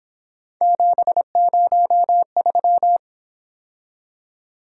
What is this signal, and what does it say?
Morse code, heard as a keyed tone, about 26 wpm, 700 Hz. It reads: 703
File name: Speed-Racing - Three Letter-Number Combinations - 22wpm